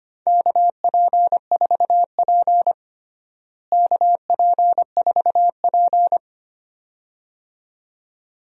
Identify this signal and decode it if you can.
Morse code, heard as a keyed tone, about 25 wpm, 700 Hz. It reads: KP4P KP4P
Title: DXCC-100 Call Signs - 4 Characters - Repeated 2x - 25wpm